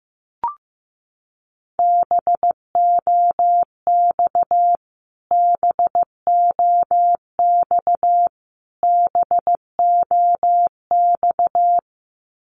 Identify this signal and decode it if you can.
Morse code, heard as a keyed tone, about 15 wpm, 700 Hz. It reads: BOX BOX BOX
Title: Top 500 Words Repeated 15wpm